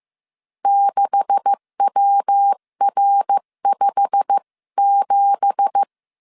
Morse code, heard as a keyed tone, about 15 wpm, 800 Hz. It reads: 6WR57
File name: CW / DET / 1 9 1500C S B 14wpm 3302